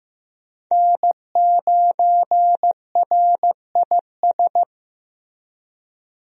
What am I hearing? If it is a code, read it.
Morse code, heard as a keyed tone, about 15 wpm, 700 Hz. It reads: N9RIS
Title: USA Call Signs - 5 Characters - Repeated 2x - 15wpm